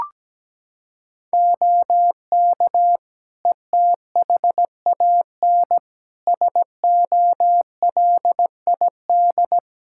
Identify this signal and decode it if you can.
Morse code, heard as a keyed tone, about 17 wpm, 700 Hz. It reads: OK ETHAN SOLID
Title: Practice QSO - Exchange 3 - More Info - 17wpm